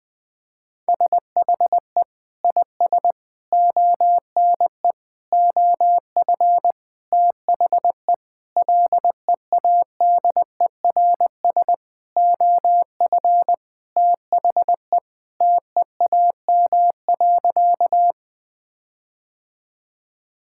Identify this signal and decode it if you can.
Morse code, heard as a keyed tone, about 20 wpm, 700 Hz. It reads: SHE IS ONE OF THE LEADERS OF THE TEAM.
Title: Sentences with 6-Word Phrases - 20wpm